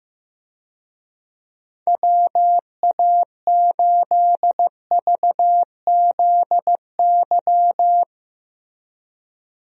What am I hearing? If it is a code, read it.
Morse code, heard as a keyed tone, about 16 wpm, 700 Hz. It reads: WA8VZY
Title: USA Call Signs - 6 Characters - Repeated 3x - 15wpm